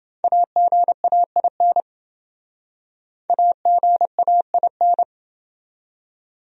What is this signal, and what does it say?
Morse code, heard as a keyed tone, about 29 wpm, 700 Hz. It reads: AGAIN AGAIN